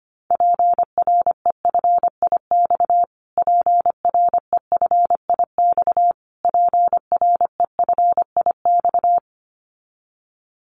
Morse code, heard as a keyed tone, about 25 wpm, 700 Hz. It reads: PREFIX PREFIX PREFIX
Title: Callsign Prefixes for Top 25 DXCC - Repeated 3x - 25wpm